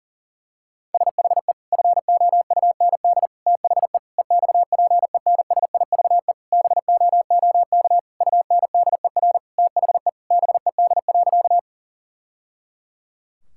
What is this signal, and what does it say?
Morse code, heard as a keyed tone, about 40 wpm, 700 Hz. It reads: SHE FOUND THE EXPENSIVE BOOK UNDER THE BED.
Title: Sentences from Top 1000 Words - Review Part 6 - 40wpm